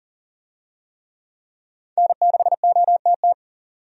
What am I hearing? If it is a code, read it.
Morse code, heard as a keyed tone, about 40 wpm, 700 Hz. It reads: N6OTT
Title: USA Call Signs - 5 Characters - 40wpm